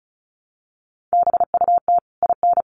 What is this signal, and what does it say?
Morse code, heard as a keyed tone, about 34 wpm, 700 Hz. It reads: BUT IN
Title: Sentences from Top 1500 Words 35wpm